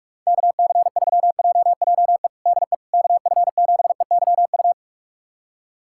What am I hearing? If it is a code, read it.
Morse code, heard as a keyed tone, about 41 wpm, 700 Hz. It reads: KC3JJE DE KF7EYU
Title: USA Callsign DE Callsign 45wpm